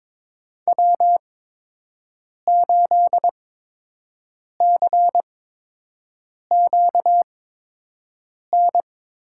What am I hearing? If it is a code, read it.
Morse code, heard as a keyed tone, about 22 wpm, spaced as for 6 wpm, 700 Hz. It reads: W8CQN
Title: USA Call Signs - 5 Characters - 8x Character Spacing - 22wpm